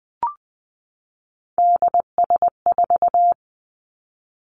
Morse code, heard as a keyed tone, about 20 wpm, 700 Hz. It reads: DS4
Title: Three Challenging Letter-Number Combinations 20wpm